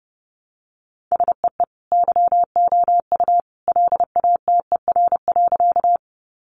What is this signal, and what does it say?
Morse code, heard as a keyed tone, about 30 wpm, 700 Hz. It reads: SEE YOU LATER.
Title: Sentences from Top 700 Words 30wpm